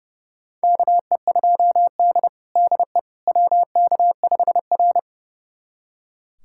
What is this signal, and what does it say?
Morse code, heard as a keyed tone, about 30 wpm, 700 Hz. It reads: KE2D DE WK5R
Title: USA Callsign DE Callsign - 4 Characters - 30wpm